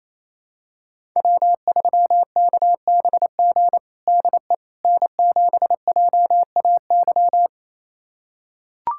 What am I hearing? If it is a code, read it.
Morse code, heard as a keyed tone, about 28 wpm, 700 Hz. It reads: W3KBG DE N7JAY
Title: USA Callsign DE Callsign - 5 Characters - 28wpm